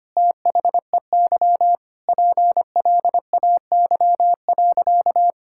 Morse code, heard as a keyed tone, about 25 wpm, 700 Hz. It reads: THEY PLAY.
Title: Sentences from Top 1000 Words - Review Part 5 - 25wpm